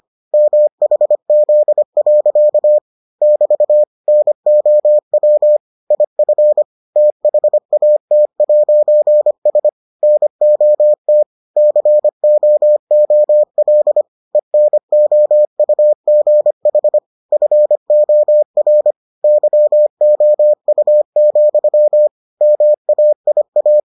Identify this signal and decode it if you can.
Morse code, heard as a keyed tone, about 25 wpm, 600 Hz. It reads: MHZ. <BT> NOW IF THAT'S NOT COOL ENOUGH FOR YOU, MAIA